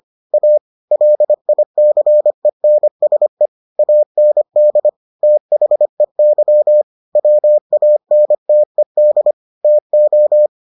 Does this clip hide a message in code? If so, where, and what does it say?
Yes: Morse code from 0.3 to 10.5 s, heard as a keyed tone, about 25 wpm, 600 Hz. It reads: A LICENSE AND THEY WANTED TO